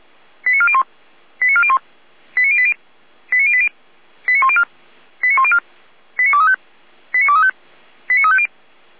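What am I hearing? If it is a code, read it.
Five-tone selective call (ZVEI), heard as quick runs of five tones, tones of about 70 ms. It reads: 89401 89401 89080 89080 89104 89104 89235 89235 89350
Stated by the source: Received on an SDR as a ZVEI five-tone call, code 89025.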